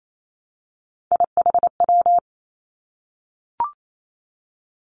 Morse code, heard as a keyed tone, about 28 wpm, 700 Hz. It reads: IHW
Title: Three Challenging Letter-Number Combinations 28wpm